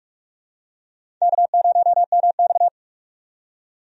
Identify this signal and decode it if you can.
Morse code, heard as a keyed tone, about 47 wpm, 700 Hz. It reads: K0MX
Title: USA Call Signs - 4 Characters - 45wpm